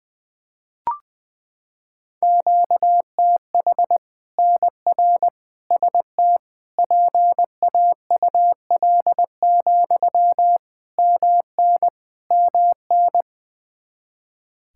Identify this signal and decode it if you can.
Morse code, heard as a keyed tone, about 20 wpm, 700 Hz. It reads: QTH NR ST PAUL, MN MN
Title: Practice QSO - Exchange 1 - Beginning - 20wpm